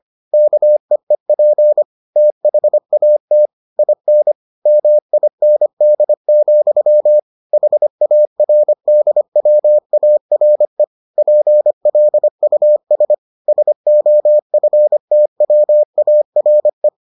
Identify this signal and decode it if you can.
Morse code, heard as a keyed tone, about 25 wpm, 600 Hz. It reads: KEEP THAT IN MIND, HARDWARE PLUS SOFTWARE